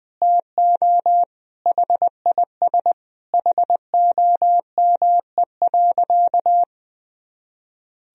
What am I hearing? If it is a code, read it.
Morse code, heard as a keyed tone, about 21 wpm, 700 Hz. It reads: TO HIS HOME.
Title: Sentences from Top 1000 Words - Review Part 1 - 20wpm